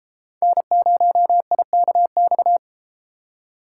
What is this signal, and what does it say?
Morse code, heard as a keyed tone, about 33 wpm, 700 Hz. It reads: N0IKX